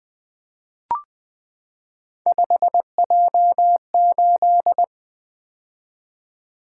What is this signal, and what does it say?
Morse code, heard as a keyed tone, about 20 wpm, 700 Hz. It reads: 5J8
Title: Three Letter-Number Combinations - 20wpm